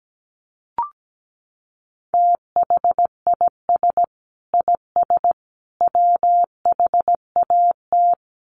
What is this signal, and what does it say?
Morse code, heard as a keyed tone, about 17 wpm, 700 Hz. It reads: THIS IS WHAT
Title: Sets of 3 Words 17wpm